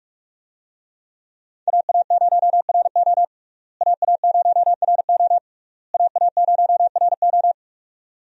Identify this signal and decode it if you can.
Morse code, heard as a keyed tone, about 46 wpm, 700 Hz. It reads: AA0RO AA0RO AA0RO
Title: USA Call Signs - 5 Characters - Repeated 3x - 45wpm